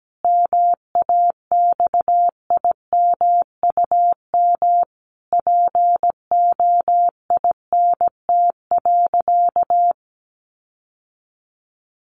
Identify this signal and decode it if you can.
Morse code, heard as a keyed tone, about 17 wpm, 700 Hz. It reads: MAXIMUM POINT.
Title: Sentences from Top 1300 Words 17wpm